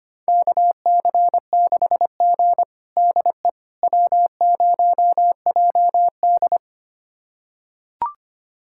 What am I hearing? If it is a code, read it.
Morse code, heard as a keyed tone, about 25 wpm, 700 Hz. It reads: KC6G DE W0JD
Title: USA Callsign DE Callsign - 4 Characters - 25wpm